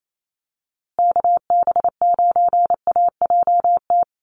Morse code, heard as a keyed tone, about 28 wpm, 700 Hz. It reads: KB9AJT
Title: USA Call Signs Repeated 3x 28wpm